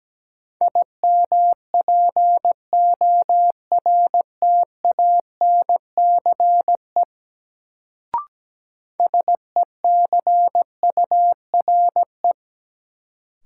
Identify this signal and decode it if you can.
Morse code, heard as a keyed tone, about 17 wpm, 700 Hz. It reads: IMPORTANCE SECURE
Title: Sentences from Top 1200 Words - New Word Drill Alt - 17wpm